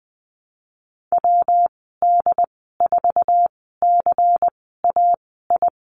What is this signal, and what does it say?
Morse code, heard as a keyed tone, about 20 wpm, spaced as for 15 wpm, 700 Hz. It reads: WD4CAI
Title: 173 Lesson 28g - Number 9 - Call Signs 20wpm